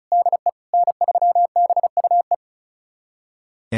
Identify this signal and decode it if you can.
Morse code, heard as a keyed tone, about 34 wpm, 700 Hz. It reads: DE N3BUE
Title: USA Callsign DE Callsign - 5 Characters - 35wpm